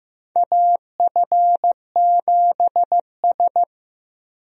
Morse code, heard as a keyed tone, about 15 wpm, 700 Hz. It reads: AF7S
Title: USA Call Signs - 4 Characters - 15wpm